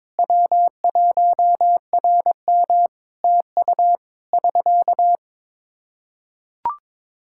Morse code, heard as a keyed tone, about 22 wpm, 700 Hz. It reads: W1RM TU <SK>
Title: Practice QSO - Exchange 4 - Ending - 22wpm